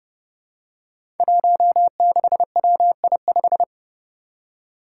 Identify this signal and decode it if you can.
Morse code, heard as a keyed tone, about 30 wpm, 700 Hz. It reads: 16WI5
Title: Five Challenging Letter-Number Combinations 30wpm